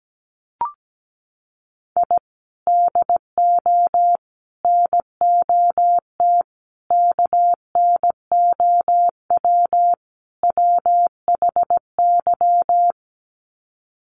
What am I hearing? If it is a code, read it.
Morse code, heard as a keyed tone, about 17 wpm, 700 Hz. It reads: I DO NOT KNOW WHY
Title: Sets of 5 Words 17wpm